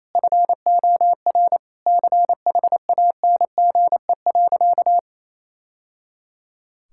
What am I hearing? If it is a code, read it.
Morse code, heard as a keyed tone, about 28 wpm, 700 Hz. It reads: FOR CHANGE.
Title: Sentences from Top 1000 Words - Review Part 6 - 28wpm